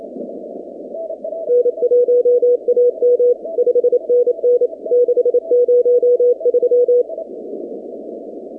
Morse code, heard as a keyed tone, about 27 wpm, 490 Hz. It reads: N1AM 5NN 603
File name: 2019-cw-wpx-kt5j-k5pi